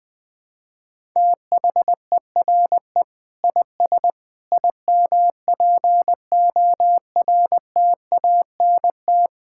Morse code, heard as a keyed tone, about 21 wpm, 700 Hz. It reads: THERE IS IMPORTANT
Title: Sentences from Top 1000 Words - Review Part 6 - 20wpm